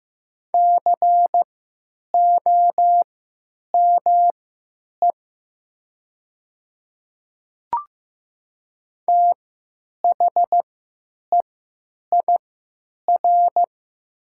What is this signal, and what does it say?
Morse code, heard as a keyed tone, about 15 wpm, spaced as for 9 wpm, 700 Hz. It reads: COME THEIR
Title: 3x Character Spacing - Top 100 Words - 15wpm